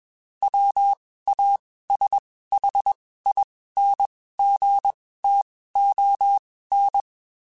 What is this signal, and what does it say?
Morse code, heard as a keyed tone, about 21 wpm, spaced as for 15 wpm, 800 Hz. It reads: WASHINGTON
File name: QSOs / QSOs-mp3s / WD0BGP 1@20